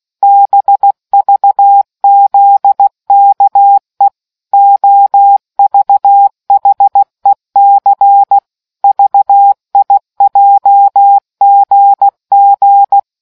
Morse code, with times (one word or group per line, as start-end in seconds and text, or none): 0.2-4.1 BVZKE
4.5-8.4 OVHEC
8.8-13.0 VIJGG